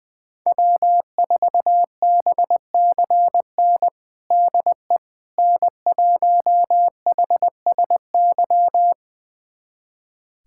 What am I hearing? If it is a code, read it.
Morse code, heard as a keyed tone, about 20 wpm, 700 Hz. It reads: W4BCN DE N1HSY
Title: USA Callsign DE Callsign - 5 Characters - 20wpm